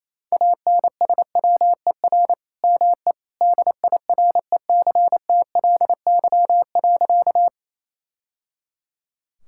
Morse code, heard as a keyed tone, about 28 wpm, 700 Hz. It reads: ANSWER ME DIRECTLY.
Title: Sentences from Top 1000 Words - Review Part 5 - 28wpm